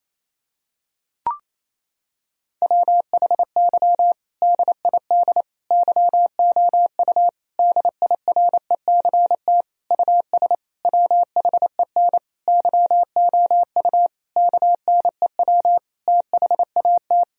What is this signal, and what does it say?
Morse code, heard as a keyed tone, about 28 wpm, 700 Hz. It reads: WHY DID YOU DIRECT US WHEN YOU KNEW THAT
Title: Sentences from Top 1000 Words - Review Part 2 - 28wpm